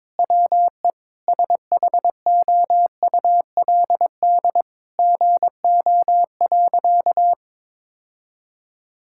Morse code, heard as a keyed tone, about 22 wpm, 700 Hz. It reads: WE SHOULD GO.
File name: practice / 22wpm / Sentences with 4-Word Phrases - 22wpm